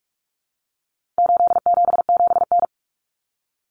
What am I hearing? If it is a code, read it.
Morse code, heard as a keyed tone, about 44 wpm, 700 Hz. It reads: 877N
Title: Common Cut Numbers - 4-Digit 45wpm